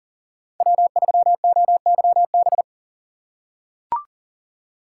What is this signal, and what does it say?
Morse code, heard as a keyed tone, about 40 wpm, 700 Hz. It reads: W3OYB
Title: USA Call Signs - 5 Characters - 4x Character Spacing - 40wpm